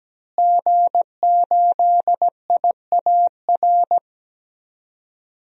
Morse code, heard as a keyed tone, about 17 wpm, 700 Hz. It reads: G8IAR
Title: DXCC-100 Call Signs - 5 Characters - Repeated 2x - 17wpm